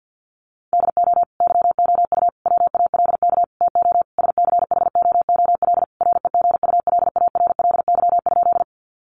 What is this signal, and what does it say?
Morse code, heard as a keyed tone, about 50 wpm, 700 Hz. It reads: DO YOU WALK TO SCHOOL REGULARLY?
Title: Sentences from Top 1400 Words 50wpm